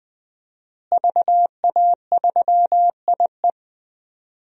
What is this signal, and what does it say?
Morse code, heard as a keyed tone, about 20 wpm, 700 Hz. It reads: VA3IE
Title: DXCC-100 Call Signs - 5 Characters - Repeated 2x - 20wpm